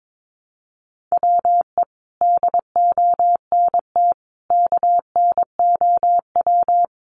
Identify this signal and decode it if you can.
Morse code, heard as a keyed tone, about 22 wpm, 700 Hz. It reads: WE DONT KNOW
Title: Sentences from Top 400 Words 22wpm